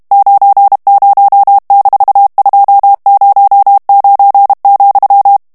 Morse code, heard as a keyed tone, about 32 wpm, 800 Hz. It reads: 90-209,
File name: ANGLAIS texte clair 3 vitesse 30 mots minute